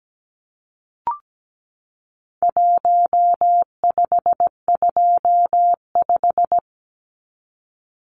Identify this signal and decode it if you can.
Morse code, heard as a keyed tone, about 17 wpm, 700 Hz. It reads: 1525
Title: Numbers - Contest Style - 4-Digit 17wpm